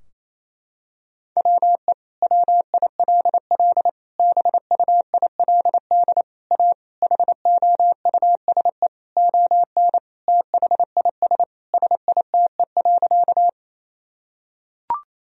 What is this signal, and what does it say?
Morse code, heard as a keyed tone, about 27 wpm, 700 Hz. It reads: WE WILL BUILD A HOUSE ON THIS SITE.
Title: Sentences from Top 1000 Words - Review Part 4 - 28wpm